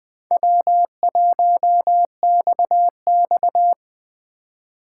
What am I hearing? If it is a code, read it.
Morse code, heard as a keyed tone, about 20 wpm, 700 Hz. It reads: W1XX
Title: Speed Racing - USA Call Signs 17wpm